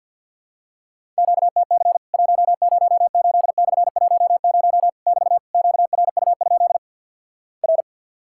Morse code, heard as a keyed tone, about 53 wpm, 700 Hz. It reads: QTC 108/10 <BT> QRU? R
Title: Worked All Europe DX Contest - QTC Exchange with Repeats - 50wpm